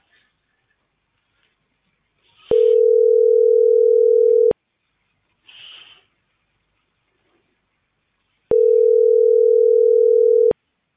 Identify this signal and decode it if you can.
Ringback tone (North America), 440 + 480 Hz.